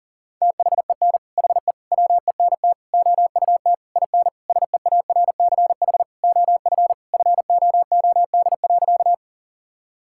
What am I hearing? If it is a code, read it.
Morse code, heard as a keyed tone, about 40 wpm, 700 Hz. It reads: THEN HE WENT OUT IN SEARCH OF FOOD.
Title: Sentences from Top 1000 Words - Review Part 5 - 40wpm